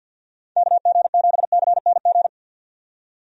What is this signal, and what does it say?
Morse code, heard as a keyed tone, about 50 wpm, 700 Hz. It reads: KG7CNG
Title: USA Call Signs - 6 Characters - 50wpm